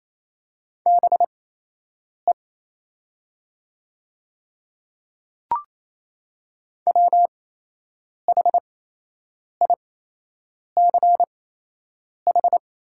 Morse code, heard as a keyed tone, about 28 wpm, spaced as for 8 wpm, 700 Hz. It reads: BE WHICH
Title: 8x Character Spacing - Top 100 Words - 28wpm